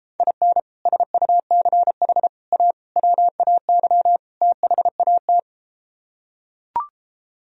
Morse code, heard as a keyed tone, about 33 wpm, 700 Hz. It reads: IN SUCH A WAY THAT